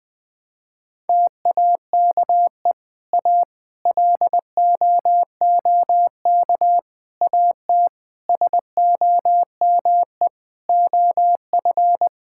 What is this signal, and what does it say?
Morse code, heard as a keyed tone, about 20 wpm, 700 Hz. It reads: TAKE A LOOK AT SOME OF
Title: Sentences with 7-Word Phrases - 20wpm